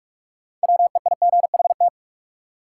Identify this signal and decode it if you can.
Morse code, heard as a keyed tone, about 44 wpm, 700 Hz. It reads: WEIGHT
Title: Sentences from Top 800 Words - New Word Drill - 45wpm